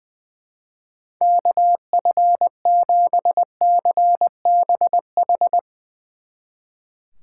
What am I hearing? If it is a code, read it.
Morse code, heard as a keyed tone, about 22 wpm, 700 Hz. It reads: KF7CBH
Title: Speed Racing - USA Call Signs 20wpm